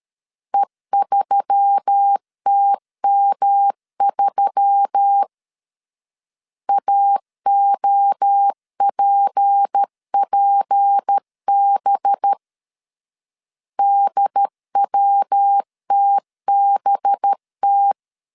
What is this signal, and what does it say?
Morse code, heard as a keyed tone, about 13 wpm, 800 Hz. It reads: E3TM3 AOPPB DWTBT
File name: CW / DET / 1 6 1500C S B 12wpm 3431